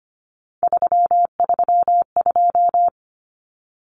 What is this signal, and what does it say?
Morse code, heard as a keyed tone, about 25 wpm, 700 Hz. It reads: 332
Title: Numbers - 3-Digit 25wpm